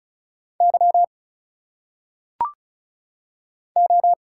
Morse code, heard as a keyed tone, about 34 wpm, 700 Hz. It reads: Y O
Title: Single Letters - Phonetic Pronunciation 35wpm